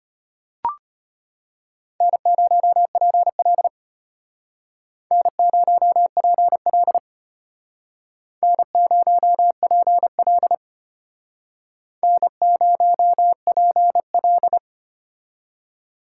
Morse code, heard as a keyed tone, about 33 wpm, 700 Hz. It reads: N0PL N0PL N0PL N0PL